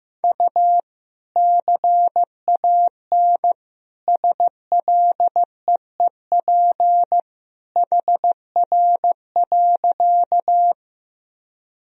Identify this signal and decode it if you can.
Morse code, heard as a keyed tone, about 15 wpm, 700 Hz. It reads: U CAN SLEEP HR.
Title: Sentences from Top 500 Words and Abbreviations 15wpm